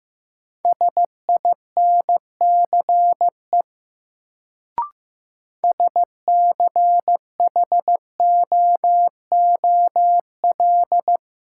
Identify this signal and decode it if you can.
Morse code, heard as a keyed tone, about 15 wpm, 700 Hz. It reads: SINCE SCHOOL
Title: Sentences from Top 300 Words - New Word Drill - 15wpm